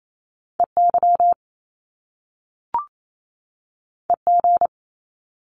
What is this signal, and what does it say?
Morse code, heard as a keyed tone, about 28 wpm, 700 Hz. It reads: EY EG
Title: Top Two Letter Combinations in Words 28wpm